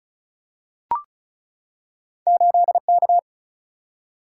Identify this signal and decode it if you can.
Morse code, heard as a keyed tone, about 36 wpm, 700 Hz. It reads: 8K
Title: Single-Letter Single-Number 35wpm